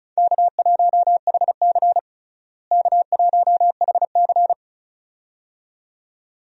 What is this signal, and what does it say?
Morse code, heard as a keyed tone, about 36 wpm, 700 Hz. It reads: K1HC K1HC
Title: USA Call Signs - 4 Characters - Repeated 2x - 35wpm